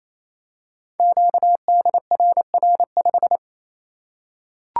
Morse code, heard as a keyed tone, about 27 wpm, 700 Hz. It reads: QDRR5
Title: Code Groups - 5 Letter:Numbers - 28wpm